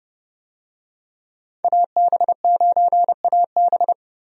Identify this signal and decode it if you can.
Morse code, heard as a keyed tone, about 32 wpm, 700 Hz. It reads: AB9AB
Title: USA Call Signs - 5 Characters - 30wpm